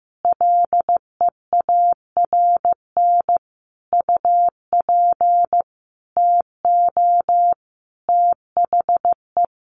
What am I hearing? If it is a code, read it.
Morse code, heard as a keyed tone, about 15 wpm, 700 Hz. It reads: LEARN UP TO THE
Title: Sentences from Top 1500 Words 15wpm